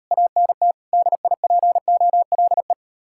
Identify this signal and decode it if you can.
Morse code, heard as a keyed tone, about 38 wpm, 700 Hz. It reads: ANT DIPOLE